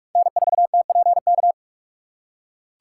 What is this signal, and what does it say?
Morse code, heard as a keyed tone, about 41 wpm, 690 Hz. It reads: N4TPK
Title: USA Call Signs - 4x Character Spacing - 45wpm